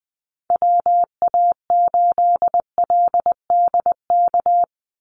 Morse code, heard as a keyed tone, about 20 wpm, 700 Hz. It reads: WA8LDK
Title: USA Call Signs 20wpm